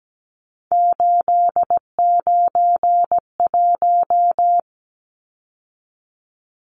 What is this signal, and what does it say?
Morse code, heard as a keyed tone, about 17 wpm, 700 Hz. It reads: 891
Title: Numbers - 3-Digit 17wpm